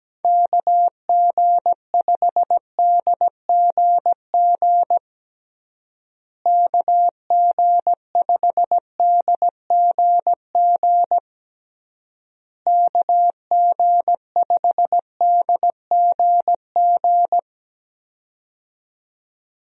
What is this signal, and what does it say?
Morse code, heard as a keyed tone, about 17 wpm, 700 Hz. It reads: KG5DGG KG5DGG KG5DGG
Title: USA Call Signs - 6 Characters - Repeated 3x - 17wpm